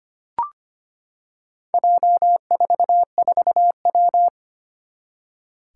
Morse code, heard as a keyed tone, about 25 wpm, 700 Hz. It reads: J44W
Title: Four Challenging Letter-Number Combinations 25wpm